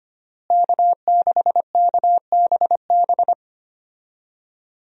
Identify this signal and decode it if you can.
Morse code, heard as a keyed tone, about 25 wpm, 700 Hz. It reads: K6KBB
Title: USA Call Signs - 5 Characters - Repeated 3x - 25wpm